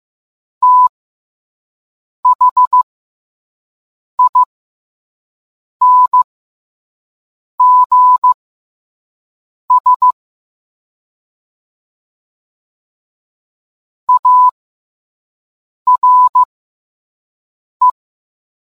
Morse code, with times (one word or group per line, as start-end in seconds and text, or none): 0.6-10.1 THINGS
14.1-17.9 ARE